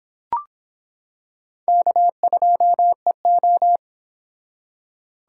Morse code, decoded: K2EO

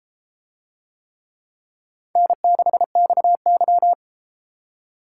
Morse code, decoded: N6XY

